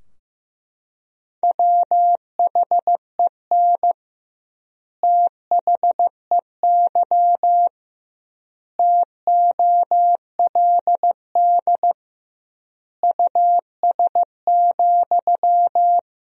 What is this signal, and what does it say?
Morse code, heard as a keyed tone, about 15 wpm, 700 Hz. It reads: WHEN THEY TOLD US,
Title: 2x Word-Spacing - Sentences from Top 400 Words - 15wpm